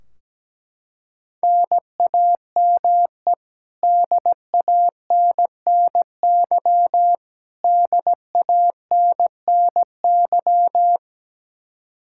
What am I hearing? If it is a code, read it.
Morse code, heard as a keyed tone, about 17 wpm, 700 Hz. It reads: NAME DANNY DANNY
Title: Practice QSO - Exchange 1 - Beginning - 17wpm